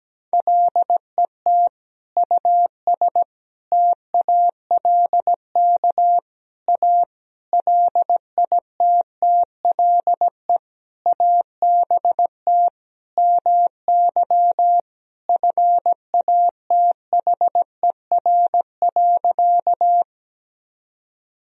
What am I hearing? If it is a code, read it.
Morse code, heard as a keyed tone, about 17 wpm, 700 Hz. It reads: LET US TALK A LITTLE ABT MY FATHER.
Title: Sentences from Top 400 Words and Abbreviations 17wpm